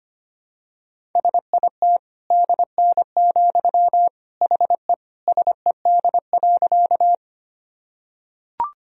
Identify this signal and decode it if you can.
Morse code, heard as a keyed tone, about 25 wpm, 700 Hz. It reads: SIT DN, HE SED.
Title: Sentences from Top 500 Words and Abbreviations 25wpm